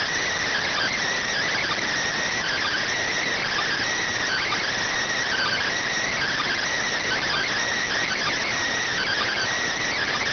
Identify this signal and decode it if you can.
Emergency Alert System (SAME) tones (header bursts), heard as harsh buzzing bursts. It is heard from 0.5 to 9.5 s.